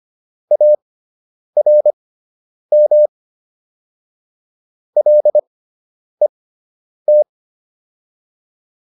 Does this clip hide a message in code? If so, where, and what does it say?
Yes: Morse code from 0.5 to 7.2 s, heard as a keyed tone, about 25 wpm, spaced as for 9 wpm, 600 Hz. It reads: ARM LET